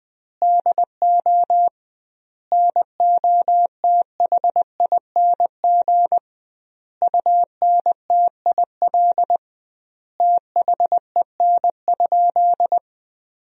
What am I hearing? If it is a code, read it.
Morse code, heard as a keyed tone, about 21 wpm, 700 Hz. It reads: DO NOTHING UNTIL THEN?
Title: 2x Word-Spacing - Sentences from Top 500 Words - 20wpm